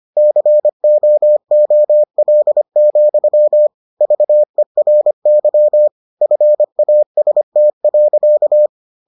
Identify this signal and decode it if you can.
Morse code, heard as a keyed tone, about 25 wpm, 600 Hz. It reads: COOL, VERY FAST.